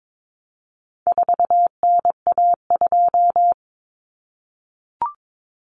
Morse code, heard as a keyed tone, about 22 wpm, 700 Hz. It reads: 4NA2
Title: Common Cut Numbers - 4-Digit 22wpm